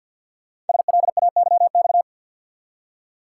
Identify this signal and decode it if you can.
Morse code, heard as a keyed tone, about 48 wpm, 700 Hz. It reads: SLAYX
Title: Code Groups - 5 Letters - 50wpm